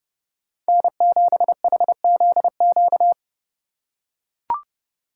Morse code, heard as a keyed tone, about 30 wpm, 700 Hz. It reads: N7HZQ
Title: USA Call Signs - 4x Character Spacing - 30wpm